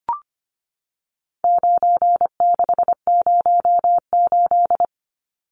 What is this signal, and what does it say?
Morse code, heard as a keyed tone, about 25 wpm, 700 Hz. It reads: 9608
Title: Numbers - 4-Digit 25wpm